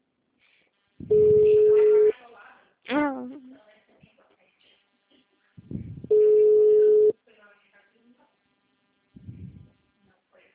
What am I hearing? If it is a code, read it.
Ringback tone (Europe, 425 Hz), with speech.
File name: out- -2024- - - .40